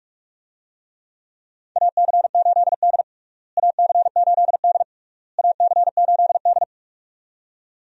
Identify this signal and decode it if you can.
Morse code, heard as a keyed tone, about 45 wpm, 700 Hz. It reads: AC8D AC8D AC8D
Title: USA Call Signs - 4 Characters - Repeated 3x - 45wpm